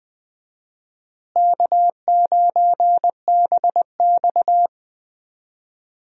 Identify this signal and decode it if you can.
Morse code, heard as a keyed tone, about 20 wpm, 700 Hz. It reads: K9BX
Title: USA Call Signs - 4 Characters - Repeated 2x - 20wpm